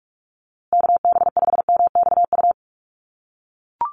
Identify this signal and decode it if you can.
Morse code, heard as a keyed tone, about 44 wpm, 700 Hz. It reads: KB5MXU
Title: USA Call Signs Repeated 2x 45wpm